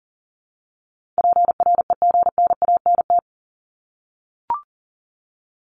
Morse code, heard as a keyed tone, about 39 wpm, 700 Hz. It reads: PREGNANT
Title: Top 751-1000 Words 40wpm